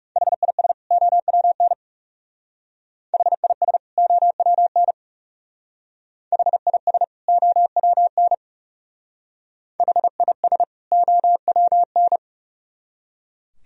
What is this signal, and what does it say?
Morse code, heard as a keyed tone, about 35 wpm, 700 Hz. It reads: HIS OWN HIS OWN HIS OWN HIS OWN
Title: Speed-Racing - Sets of 2 Words 30wpm